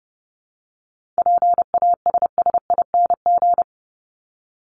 Morse code, heard as a keyed tone, about 30 wpm, 700 Hz. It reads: PASSING
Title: ING Suffix - Top 100 Words with Base Words 30wpm